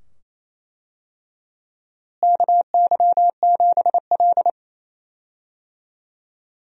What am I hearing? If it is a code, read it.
Morse code, heard as a keyed tone, about 27 wpm, 700 Hz. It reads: KY7L
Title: USA Call Signs - 4 Characters - Repeated 2x - 28wpm